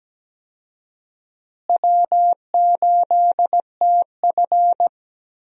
Morse code, heard as a keyed tone, about 17 wpm, 700 Hz. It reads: W8TF
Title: USA Call Signs - 4 Characters - Repeated 3x - 17wpm